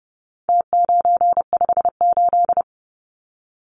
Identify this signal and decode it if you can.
Morse code, heard as a keyed tone, about 30 wpm, 700 Hz. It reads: T958
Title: Numbers - Contest Style - 4-Digit 30wpm